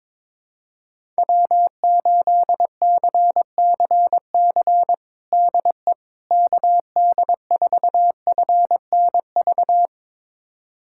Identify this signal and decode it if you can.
Morse code, heard as a keyed tone, about 22 wpm, 700 Hz. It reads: W8CCC DE KD4FNV
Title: USA Callsign DE Callsign 22wpm